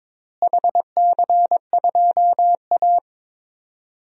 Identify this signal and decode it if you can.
Morse code, heard as a keyed tone, about 22 wpm, 700 Hz. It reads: HC2A
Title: DXCC-100 Call Signs - 4 Characters - Repeated 3x - 22wpm